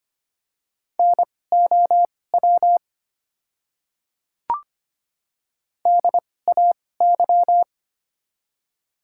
Morse code, heard as a keyed tone, about 25 wpm, spaced as for 18 wpm, 700 Hz. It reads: NOW DAY
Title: 2x Character Spacing - Top 200 Words - 25wpm